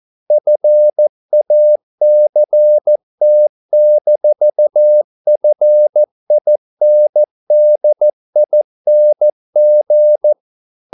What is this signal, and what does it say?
Morse code, heard as a keyed tone, about 14 wpm, 600 Hz. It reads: FACT-FINDING